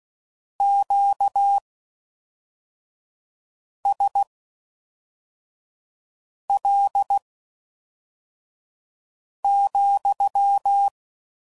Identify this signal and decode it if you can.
Morse code, heard as a keyed tone, about 16 wpm, spaced as for 4 wpm, 800 Hz. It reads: QSL,